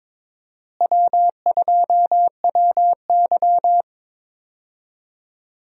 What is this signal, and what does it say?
Morse code, heard as a keyed tone, about 22 wpm, 700 Hz. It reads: W2WY
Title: USA Call Signs - 4 Characters - Repeated 3x - 22wpm